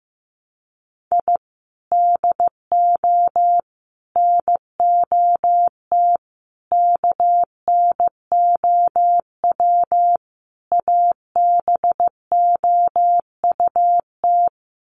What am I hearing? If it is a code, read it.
Morse code, heard as a keyed tone, about 15 wpm, 700 Hz. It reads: I DO NOT KNOW ABOUT
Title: Sets of 5 Words 15wpm